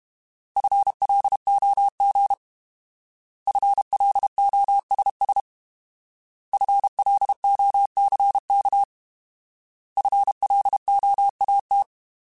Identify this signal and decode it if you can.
Morse code, heard as a keyed tone, about 32 wpm, 800 Hz. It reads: FLOG FLOSS FLOCK FLOAT
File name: Alphabetical / Alphabetical mp3s / Fs @30